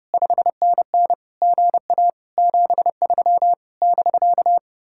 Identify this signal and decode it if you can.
Morse code, heard as a keyed tone, about 29 wpm, 700 Hz. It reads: 5NN GA 73 <BK>